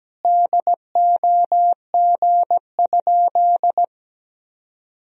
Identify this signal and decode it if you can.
Morse code, heard as a keyed tone, about 17 wpm, 700 Hz. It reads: DOG?
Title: Sentences from Top 1000 Words - Review Part 2 - 17wpm